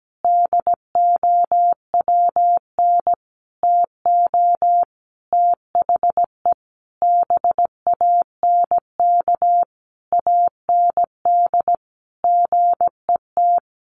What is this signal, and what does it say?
Morse code, heard as a keyed tone, about 17 wpm, 700 Hz. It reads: DOWN TO THE BANK AND GET
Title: Sentences from Top 1500 Words 17wpm